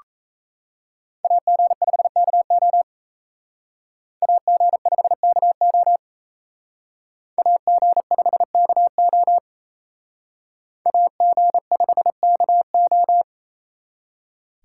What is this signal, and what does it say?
Morse code, heard as a keyed tone, about 35 wpm, 700 Hz. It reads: AG5KO AG5KO AG5KO AG5KO